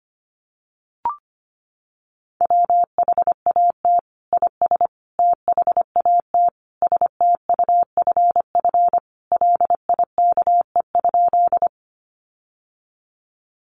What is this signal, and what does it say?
Morse code, heard as a keyed tone, about 25 wpm, 700 Hz. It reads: WHAT IS THAT STUFF LIKE?
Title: Sentences from Top 1400 Words 25wpm